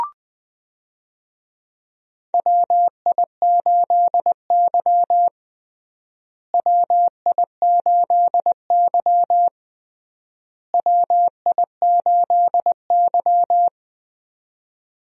Morse code, heard as a keyed tone, about 21 wpm, 700 Hz. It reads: WI8Y WI8Y WI8Y
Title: USA Call Signs - 4 Characters - Repeated 3x - 20wpm